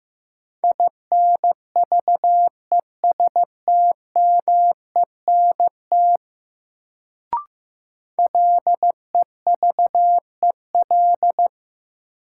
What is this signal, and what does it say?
Morse code, heard as a keyed tone, about 15 wpm, 700 Hz. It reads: INVESTMENT LEVEL
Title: Sentences from Top 700 Words - New Word Drill - 15wpm